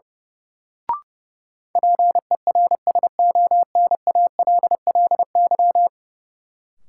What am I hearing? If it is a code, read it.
Morse code, heard as a keyed tone, about 32 wpm, 700 Hz. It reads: PERSONALLY
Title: Sentences from Top 1400 Words - New Word Drill Alt - 30wpm